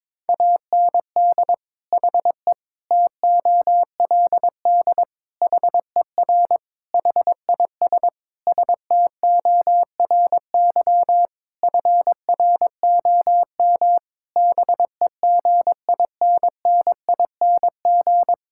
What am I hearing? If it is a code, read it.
Morse code, heard as a keyed tone, about 22 wpm, 700 Hz. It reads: AND HE TOLD HER HIS STORY FROM BEGINNING
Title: Sentences from Top 1000 Words - Review Part 5 - 22wpm